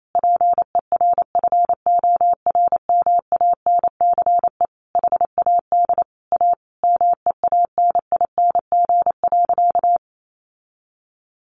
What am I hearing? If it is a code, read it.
Morse code, heard as a keyed tone, about 28 wpm, 700 Hz. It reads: PERFORMANCE HAD A MEANING.
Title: Sentences from Top 1400 Words 28wpm